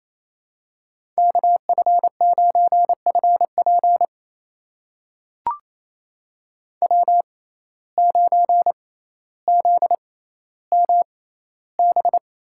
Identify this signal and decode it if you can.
Morse code, heard as a keyed tone, about 28 wpm, spaced as for 14 wpm, 700 Hz. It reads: KF9FP W9ZMB
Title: USA Call Signs - 5 Characters - 6x Character Spacing - 28wpm